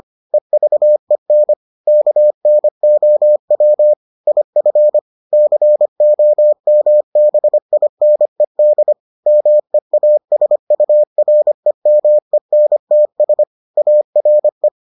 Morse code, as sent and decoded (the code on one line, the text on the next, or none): . ...- . -. / -.- -. --- .-- / .. ..-. / -.-. --- -- -... .. -. -.. / -- . .- ... ..- .-. -- . -. - ... / .- .-. .
EVEN KNOW IF COMBINED MEASUREMENTS ARE